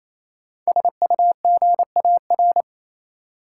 Morse code, heard as a keyed tone, about 28 wpm, 700 Hz. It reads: SUGAR
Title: Sentences from Top 1200 Words - New Word Drill Alt - 28wpm